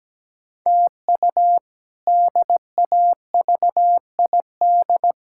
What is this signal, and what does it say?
Morse code, heard as a keyed tone, about 17 wpm, 700 Hz. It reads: TU DAVID
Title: K1USN SST QSO Practice - 17wpm